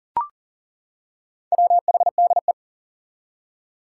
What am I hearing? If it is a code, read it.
Morse code, heard as a keyed tone, about 38 wpm, 700 Hz. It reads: WHDE
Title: Four Challenging Letter-Number Combinations 40wpm